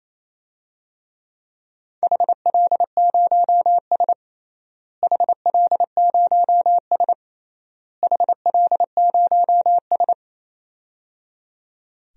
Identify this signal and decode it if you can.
Morse code, heard as a keyed tone, about 27 wpm, 700 Hz. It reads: HL0S HL0S HL0S
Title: DXCC-100 Call Signs - 4 Characters - Repeated 3x - 28wpm